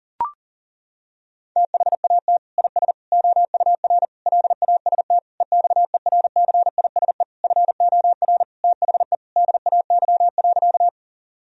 Morse code, heard as a keyed tone, about 40 wpm, 700 Hz. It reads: THAT IS OUR LAST EXERCISE FOR THE DAY.
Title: Sentences from Top 1000 Words - Review Part 4 - 40wpm